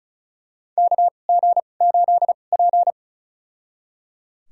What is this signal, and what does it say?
Morse code, heard as a keyed tone, about 36 wpm, spaced as for 25 wpm, 700 Hz. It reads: KG8P
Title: USA Call Signs - 4 Characters - 2x Character Spacing - 35wpm